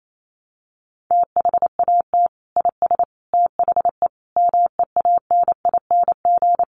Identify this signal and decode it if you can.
Morse code, heard as a keyed tone, about 28 wpm, 700 Hz. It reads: THAT IS THE MEANING
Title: Sentences from Top 900 Words 28wpm